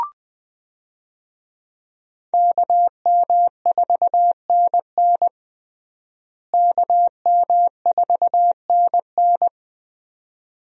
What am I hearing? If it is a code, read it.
Morse code, heard as a keyed tone, about 20 wpm, 700 Hz. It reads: KM4NN KM4NN
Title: USA Call Signs - 5 Characters - Repeated 2x - 20wpm